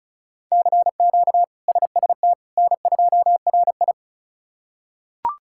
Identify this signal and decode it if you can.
Morse code, heard as a keyed tone, about 36 wpm, 700 Hz. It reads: CQ SST N2RI
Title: K1USN SST QSO Practice - 35wpm